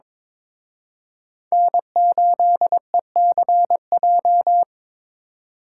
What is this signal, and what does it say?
Morse code, heard as a keyed tone, about 26 wpm, 700 Hz. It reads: N8ECJ
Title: Speed Racing - USA 1x3 Call Signs 22wpm